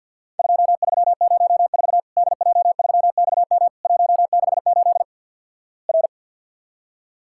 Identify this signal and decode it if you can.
Morse code, heard as a keyed tone, about 50 wpm, 700 Hz. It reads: 2304 DJ3XM 168 R
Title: Worked All Europe DX Contest - QTC Exchange - 50wpm